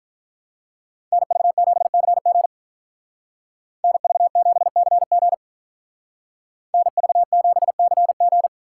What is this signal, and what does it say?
Morse code, heard as a keyed tone, about 43 wpm, 700 Hz. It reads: NV7CG NV7CG NV7CG
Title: Speed Racing - USA 2x2 Call Signs 35wpm